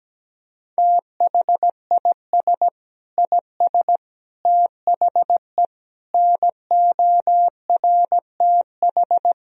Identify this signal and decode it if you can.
Morse code, heard as a keyed tone, about 17 wpm, 700 Hz. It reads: THIS IS THE NORTH
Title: Sentences from Top 1000 Words - Review Part 1 - 17wpm